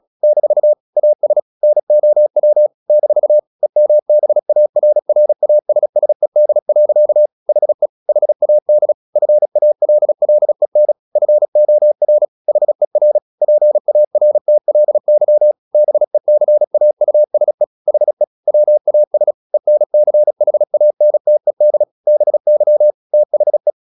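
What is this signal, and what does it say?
Morse code, heard as a keyed tone, about 36 wpm, 600 Hz. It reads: - AS NOW - EMBARRASSED. HE HAD FALLEN FOR HER PARTLY BECAUSE HE WAS ENCHANTED BY THE
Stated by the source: Morse practice at 36 wpm, text of Ken Follett's novel Never, from CW med Gnister.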